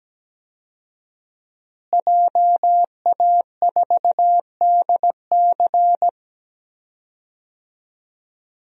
Morse code, heard as a keyed tone, about 17 wpm, 700 Hz. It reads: JA4DC
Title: DXCC-100 Call Signs - 5 Characters - Repeated 2x - 17wpm